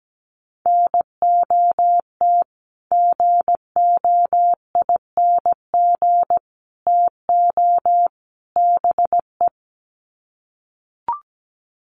Morse code, heard as a keyed tone, about 17 wpm, 700 Hz. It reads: NOT GOING TO BE
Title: Sets of 4 Words 17wpm